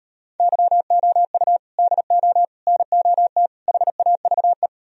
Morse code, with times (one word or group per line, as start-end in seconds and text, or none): 0.4-1.6 YOU
1.8-2.4 DO
2.7-3.5 NOT
3.7-4.7 HAVE